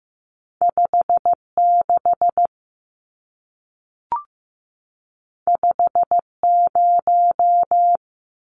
Morse code, heard as a keyed tone, about 15 wpm, 700 Hz. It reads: 56 50
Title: Numbers - 2-Digit 15wpm